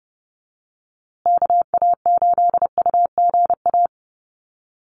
Morse code, heard as a keyed tone, about 30 wpm, 700 Hz. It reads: KA8UGA
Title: USA Call Signs Repeated 2x 30wpm